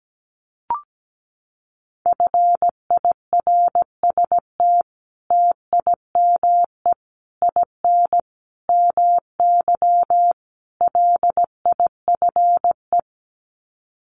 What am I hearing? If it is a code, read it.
Morse code, heard as a keyed tone, about 17 wpm, 700 Hz. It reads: FIRST TIME IN MY LIFE
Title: Sets of 5 Words Encore2 17wpm